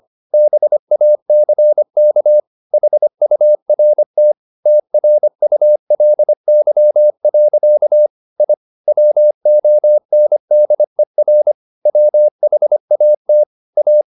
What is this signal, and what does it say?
Morse code, heard as a keyed tone, about 25 wpm, 600 Hz. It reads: BACK HURT TRULY. I WONDER WHAT A